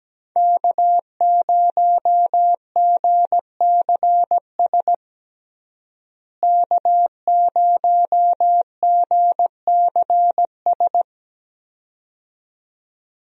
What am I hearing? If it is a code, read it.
Morse code, heard as a keyed tone, about 17 wpm, 700 Hz. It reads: K0GCS K0GCS
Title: USA Call Signs - 5 Characters - Repeated 2x - 17wpm